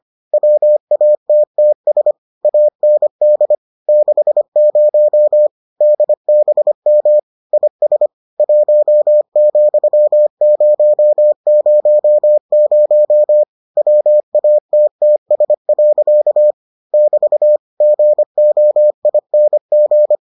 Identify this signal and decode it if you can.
Morse code, heard as a keyed tone, about 25 wpm, 600 Hz. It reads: WATTS AND 60 DBM IS 1,000 WATTS. <BT> GOING